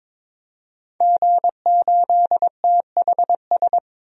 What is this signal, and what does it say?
Morse code, heard as a keyed tone, about 22 wpm, 700 Hz. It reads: G8THS
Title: DXCC-100 Call Signs - 5 Characters - 22wpm